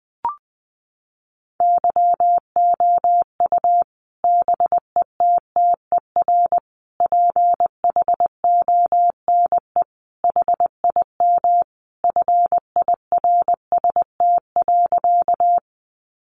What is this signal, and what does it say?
Morse code, heard as a keyed tone, about 20 wpm, 700 Hz. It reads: YOU BETTER PHONE HIM FIRST.
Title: Sentences from Top 1100 Words 20wpm